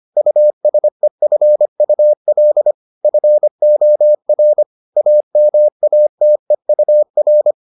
Morse code, heard as a keyed tone, about 25 wpm, 600 Hz. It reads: USEFUL FOR AMATEUR